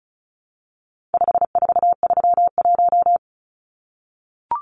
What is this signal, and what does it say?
Morse code, heard as a keyed tone, about 34 wpm, 700 Hz. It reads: 5431
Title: Numbers - 4-Digit 35wpm